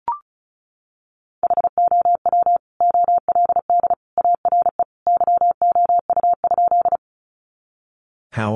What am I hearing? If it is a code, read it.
Morse code, heard as a keyed tone, about 34 wpm, 700 Hz. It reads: HOW OLD ARE YOU?
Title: Sentences from Top 200 Words 35wpm